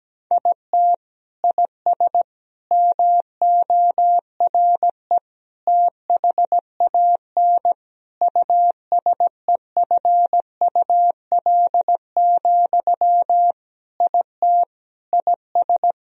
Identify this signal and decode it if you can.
Morse code, heard as a keyed tone, about 17 wpm, 700 Hz. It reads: IT IS MORE THAN USEFUL, IT IS
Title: Sentences from Top 1000 Words - Review Part 4 - 17wpm